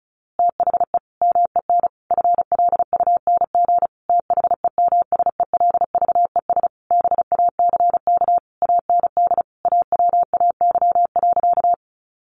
Morse code, heard as a keyed tone, about 34 wpm, 700 Hz. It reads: THE MEN FLUNG THEMSELVES BACK AND AWAY.
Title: A Sound Of Thunder 35wpm